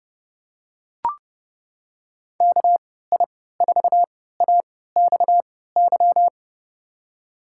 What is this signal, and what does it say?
Morse code, heard as a keyed tone, about 31 wpm, spaced as for 17 wpm, 700 Hz. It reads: KI4AXY
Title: USA Call Signs - 3x Character Spacing - 30wpm